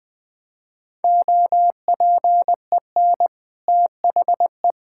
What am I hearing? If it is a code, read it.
Morse code, heard as a keyed tone, about 20 wpm, 700 Hz. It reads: OPEN THE
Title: Easy Sentences with Hints - 20wpm